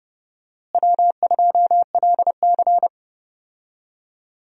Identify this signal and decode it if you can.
Morse code, heard as a keyed tone, about 30 wpm, 700 Hz. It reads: W2LC
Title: USA Call Signs - 4 Characters - Repeated 2x - 30wpm